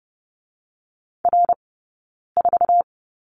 Morse code, heard as a keyed tone, about 32 wpm, 700 Hz. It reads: R 4
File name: Lesson 19d - Letter w - Two Characters 30wpm